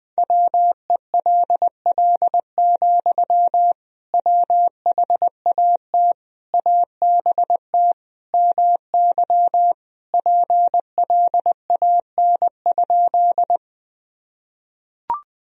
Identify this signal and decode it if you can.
Morse code, heard as a keyed tone, about 21 wpm, 700 Hz. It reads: WELL, WHAT ABT MY PLAN?
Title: Sentences from Top 500 Words and Abbreviations 20wpm